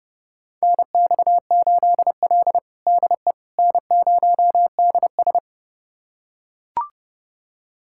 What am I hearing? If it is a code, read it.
Morse code, heard as a keyed tone, about 30 wpm, 700 Hz. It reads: NX8L DE N0DS
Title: USA Callsign DE Callsign - 4 Characters - 30wpm